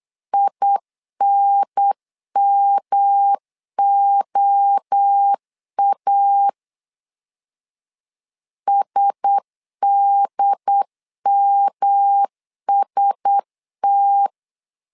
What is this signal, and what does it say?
Morse code, heard as a keyed tone, about 8 wpm, 800 Hz. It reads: INMOA SDMST
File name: LEZIONI / 4 Q1S 8wpm 2710